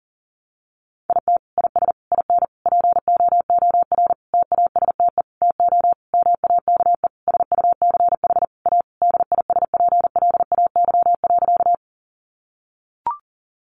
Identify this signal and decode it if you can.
Morse code, heard as a keyed tone, about 37 wpm, 700 Hz. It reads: IT IS IN POOR TASTE TO MAKE SUCH A DISPLAY.
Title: Sentences from Top 1200 Words 40wpm